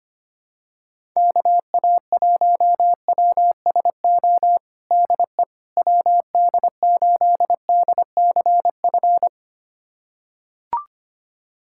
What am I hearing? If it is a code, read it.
Morse code, heard as a keyed tone, about 25 wpm, 700 Hz. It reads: KA1WSO DE WD8DCF
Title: USA Callsign DE Callsign - 6 Characters - 25wpm